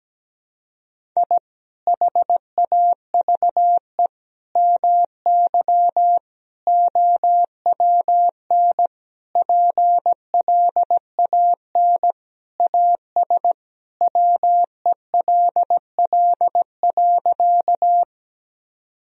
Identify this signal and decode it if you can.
Morse code, heard as a keyed tone, about 17 wpm, 700 Hz. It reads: I HAVE MY OWN PLAN AS WELL.
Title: Sentences from Top 1000 Words - Review Part 2 - 17wpm